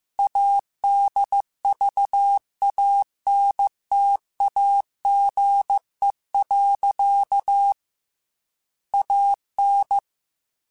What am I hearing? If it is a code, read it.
Morse code, heard as a keyed tone, about 15 wpm, 800 Hz. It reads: ADVANTAGE. AN
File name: Technical Subjects / Technical Subjects-mp3s / Sw Pwr Sup @13